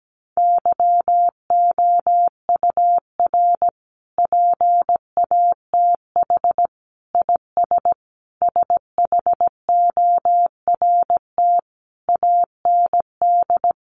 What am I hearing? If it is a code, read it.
Morse code, heard as a keyed tone, about 17 wpm, 700 Hz. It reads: YOUR PATH IS SHORT AND
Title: Sentences from Top 1400 Words 17wpm